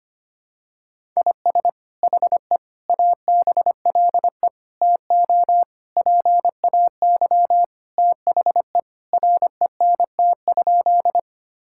Morse code, heard as a keyed tone, about 25 wpm, 700 Hz. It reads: IS HE ABLE TO PAY THE RENT?
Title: Sentences from Top 1000 Words - Review Part 5 - 25wpm